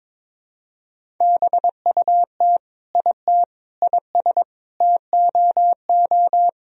Morse code, heard as a keyed tone, about 22 wpm, 700 Hz. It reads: BUT IT IS TOO
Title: Sentences from Top 1000 Words - Review Part 1 - 22wpm